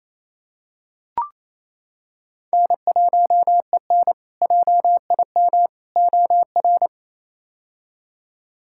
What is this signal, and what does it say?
Morse code, heard as a keyed tone, about 28 wpm, 700 Hz. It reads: N1EN JIM OR
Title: K1USN SST QSO Practice - 28wpm